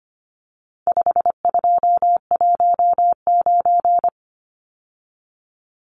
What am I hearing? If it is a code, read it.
Morse code, heard as a keyed tone, about 25 wpm, 700 Hz. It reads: 5219
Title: Numbers - 4-Digit 25wpm